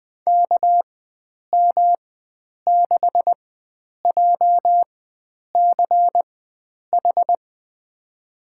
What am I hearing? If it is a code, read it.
Morse code, heard as a keyed tone, about 20 wpm, spaced as for 9 wpm, 700 Hz. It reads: KM6JCH
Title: USA Call Signs - 6 Characters - 4x Character Spacing - 20wpm